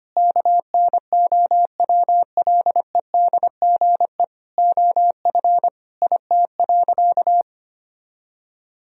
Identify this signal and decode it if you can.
Morse code, heard as a keyed tone, about 25 wpm, 700 Hz. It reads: KNOWLEDGE OF IT.
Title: Sentences from Top 1000 Words - Review Part 3 - 25wpm